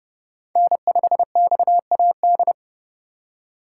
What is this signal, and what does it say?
Morse code, heard as a keyed tone, about 30 wpm, 700 Hz. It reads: N5XAD